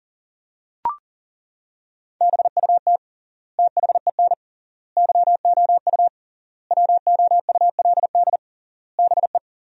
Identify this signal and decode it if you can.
Morse code, heard as a keyed tone, about 40 wpm, 700 Hz. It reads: BUT THEN YOU WOULD BE
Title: 3x Word-Spacing - Sentences from Top 300 Words - 40wpm